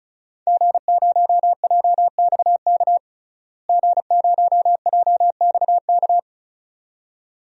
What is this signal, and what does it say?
Morse code, heard as a keyed tone, about 36 wpm, 700 Hz. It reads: G0JXK G0JXK
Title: DXCC-100 Call Signs - 5 Characters - Repeated 2x - 35wpm